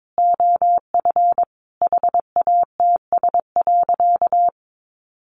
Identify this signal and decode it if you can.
Morse code, heard as a keyed tone, about 22 wpm, 700 Hz. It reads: OF HATS.
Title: Easy Phrases 22wpm